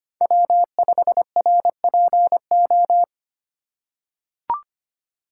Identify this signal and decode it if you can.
Morse code, heard as a keyed tone, about 25 wpm, 700 Hz. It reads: W5RPO
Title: USA Call Signs - 5 Characters - 4x Character Spacing - 25wpm